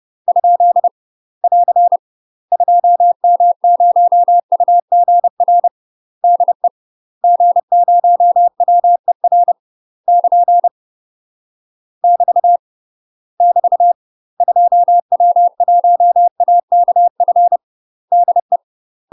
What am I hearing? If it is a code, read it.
Morse code, heard as a keyed tone, about 30 wpm, 700 Hz. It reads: ? <AR> 2M0UGR DE G0WER <KN> <BT> <BT> 2W1AKF DE
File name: UK MASTER QSOs TRAINING/30WPM